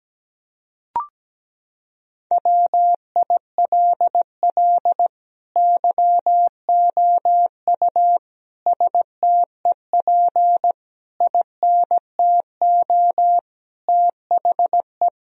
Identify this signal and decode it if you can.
Morse code, heard as a keyed tone, about 17 wpm, 700 Hz. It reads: WILL YOU STEP INTO THE
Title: Sentences from Top 1000 Words - Review Part 2 - 17wpm